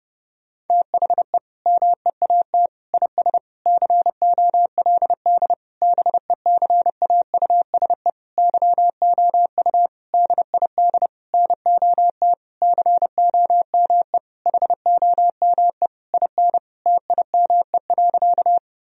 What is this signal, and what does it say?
Morse code, heard as a keyed tone, about 32 wpm, 700 Hz. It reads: THE MEAT IS COLD BECAUSE YOU DID NOT COME HOME IN TIME.
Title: Sentences from Top 1000 Words - Review Part 3 - 30wpm